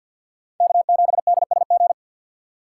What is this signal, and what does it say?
Morse code, heard as a keyed tone, about 47 wpm, 700 Hz. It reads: K7DSG
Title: USA Call Signs - 5 Characters - 3x Character Spacing - 50wpm